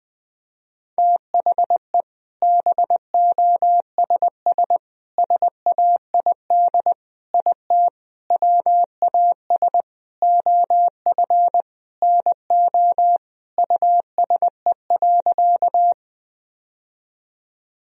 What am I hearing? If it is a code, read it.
Morse code, heard as a keyed tone, about 20 wpm, 700 Hz. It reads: THE BOSS SAID IT WAS OF NO USE.
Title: Sentences from Top 1000 Words - Review Part 4 - 20wpm